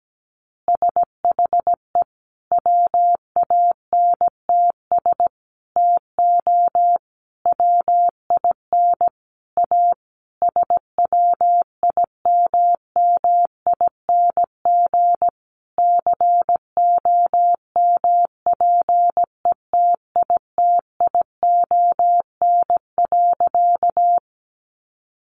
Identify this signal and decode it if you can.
Morse code, heard as a keyed tone, about 17 wpm, 700 Hz. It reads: SHE WANTS TO WIN A SWIMMING COMPETITION.
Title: Additional Easy Phrases 17wpm